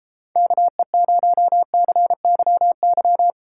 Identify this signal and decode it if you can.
Morse code, heard as a keyed tone, about 32 wpm, 700 Hz. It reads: KE0CYY